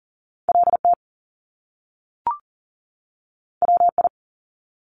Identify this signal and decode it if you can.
Morse code, heard as a keyed tone, about 41 wpm, 700 Hz. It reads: LT WI
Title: Top Two Letter Combinations in Words 40wpm